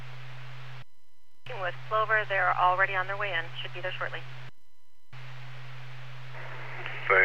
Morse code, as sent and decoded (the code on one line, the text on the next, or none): none